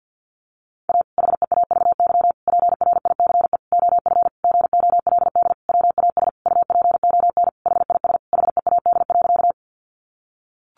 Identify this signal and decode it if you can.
Morse code, heard as a keyed tone, about 50 wpm, 700 Hz. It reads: A HEAVY PRICE OF GOLD WAS UPON HIS HEAD.
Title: Sentences from Top 1100 Words 50wpm